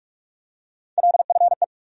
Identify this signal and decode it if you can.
Morse code, heard as a keyed tone, about 44 wpm, 700 Hz. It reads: LFE
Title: USA Call Sign Postfixes - 3 Letters 45wpm